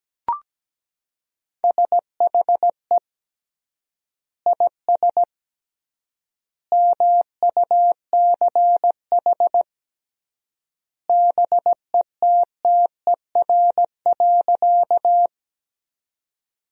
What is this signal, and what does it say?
Morse code, heard as a keyed tone, about 17 wpm, 700 Hz. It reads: SHE IS MUCH BETTER.
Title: 3x Word-Spacing - Sentences from Top 400 Words - 17wpm